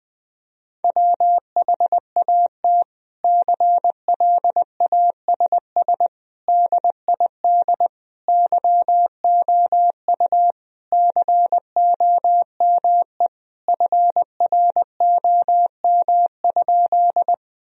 Morse code, heard as a keyed tone, about 20 wpm, 700 Hz. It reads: WHAT CLASS DID YOU COME FROM?
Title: Sentences from Top 1000 Words - Review Part 2 - 20wpm